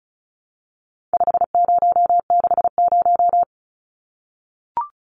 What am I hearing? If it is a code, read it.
Morse code, heard as a keyed tone, about 34 wpm, 700 Hz. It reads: 5060
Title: Numbers - 4-Digit 35wpm